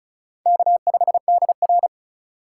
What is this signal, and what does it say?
Morse code, heard as a keyed tone, about 36 wpm, 700 Hz. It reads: K5DR
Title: USA Call Signs - 4 Characters - 2x Character Spacing - 35wpm